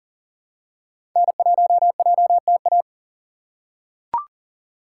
Morse code, heard as a keyed tone, about 40 wpm, 700 Hz. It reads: N1JTA
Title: USA Call Signs - 5 Characters - 3x Character Spacing - 40wpm